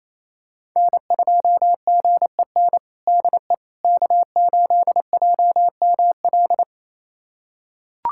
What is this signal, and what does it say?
Morse code, heard as a keyed tone, about 28 wpm, 700 Hz. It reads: N2GEN DE K8JML
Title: USA Callsign DE Callsign - 5 Characters - 28wpm